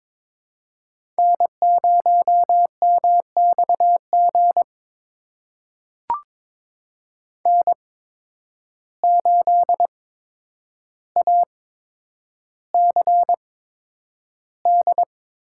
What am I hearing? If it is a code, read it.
Morse code, heard as a keyed tone, about 22 wpm, spaced as for 9 wpm, 700 Hz. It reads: N0MXG N8ACD
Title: USA Call Signs - 5 Characters - 8x Character Spacing - 22wpm